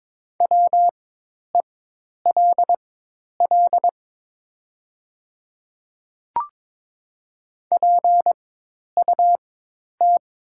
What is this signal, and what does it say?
Morse code, heard as a keyed tone, about 22 wpm, spaced as for 10 wpm, 700 Hz. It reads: WELL PUT
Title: 4x Character Spacing - Top 200 Words - 22wpm